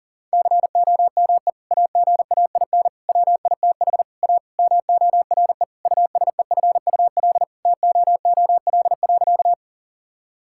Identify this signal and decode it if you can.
Morse code, heard as a keyed tone, about 40 wpm, 700 Hz. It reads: COME AGAIN WITH A MORE USEFUL TOOL.
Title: Sentences from Top 1000 Words - Review Part 6 - 40wpm